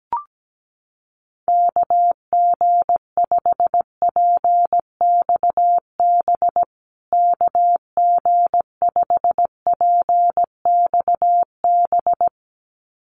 Morse code, heard as a keyed tone, about 17 wpm, 700 Hz. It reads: KG5PXB KG5PXB
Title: USA Call Signs Repeated 2x 17wpm